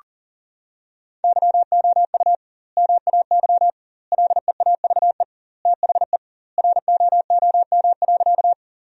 Morse code, heard as a keyed tone, about 40 wpm, 700 Hz. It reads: YOU MAY LEAVE THE ROOM.
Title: 2x Word-Spacing - Sentences from Top 400 Words - 40wpm